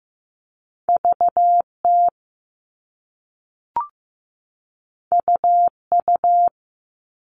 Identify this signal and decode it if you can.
Morse code, heard as a keyed tone, about 15 wpm, 700 Hz. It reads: VT UU
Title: All Cut Numbers - 2-Digit 15wpm